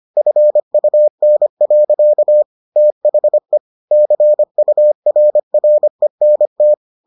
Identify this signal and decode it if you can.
Morse code, heard as a keyed tone, about 25 wpm, 600 Hz. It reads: FUN. THE CURRENT